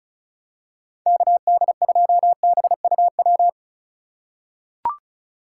Morse code, heard as a keyed tone, about 36 wpm, 700 Hz. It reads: KD2BUW
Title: USA Call Signs - 6 Characters - 3x Character Spacing - 35wpm